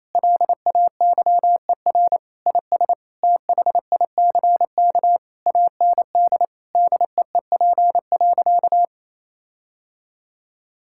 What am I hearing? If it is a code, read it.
Morse code, heard as a keyed tone, about 28 wpm, 700 Hz. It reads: LAYER IS THICK AND DEEP.
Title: Sentences from Top 1600 Words 28wpm